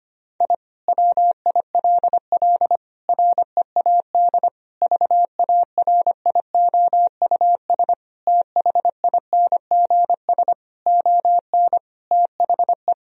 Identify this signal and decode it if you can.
Morse code, heard as a keyed tone, about 25 wpm, 700 Hz. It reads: I WILL READ VARIOUS THINGS ON THE
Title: Sentences from Top 1000 Words - Review Part 4 - 25wpm